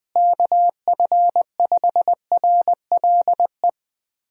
Morse code, heard as a keyed tone, about 20 wpm, 700 Hz. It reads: KF5RLE
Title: USA Call Signs - 6 Characters - Repeated 3x - 20wpm